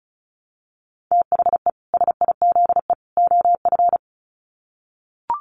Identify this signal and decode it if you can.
Morse code, heard as a keyed tone, about 34 wpm, 700 Hz. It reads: THE SIZE OF
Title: Sets of 3 Words 35wpm